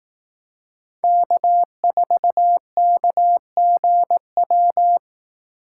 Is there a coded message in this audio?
Yes: Morse code from 1.0 to 5.0 s, heard as a keyed tone, about 20 wpm, 700 Hz. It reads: K4KGW